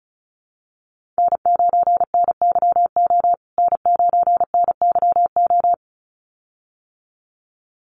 Morse code, heard as a keyed tone, about 34 wpm, 700 Hz. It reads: N9NYO N9NYO
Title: USA Call Signs Repeated 2x 35wpm